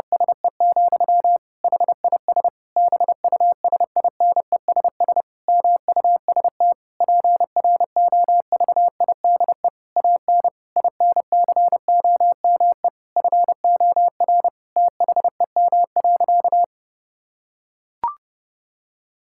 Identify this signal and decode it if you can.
Morse code, heard as a keyed tone, about 30 wpm, 700 Hz. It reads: SE, HIS BUSINESS MUST PROVIDE AN INCOME FOR THEM.
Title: Sentences from Top 1000 Words - Review Part 5 - 30wpm